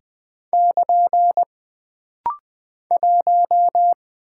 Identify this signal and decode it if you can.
Morse code, heard as a keyed tone, about 20 wpm, 700 Hz. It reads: <KN> 1
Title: Single Character (Letter:Number:Punctuation:Prosign) - Rapid-Fire - 20wpm